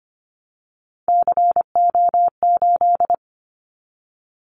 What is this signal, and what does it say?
Morse code, heard as a keyed tone, about 25 wpm, 700 Hz. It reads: CO8
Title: Callsign Prefixes for Top 100 DXCC - 25wpm